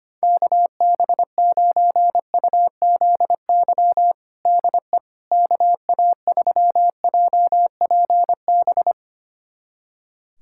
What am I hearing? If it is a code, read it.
Morse code, heard as a keyed tone, about 25 wpm, 700 Hz. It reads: KB9UZY DE KA3JPB
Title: USA Callsign DE Callsign - 6 Characters - 25wpm